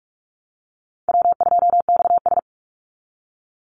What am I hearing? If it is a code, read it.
Morse code, heard as a keyed tone, about 44 wpm, 700 Hz. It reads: W2XS
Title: Call Signs Repeated 3x - 45wpm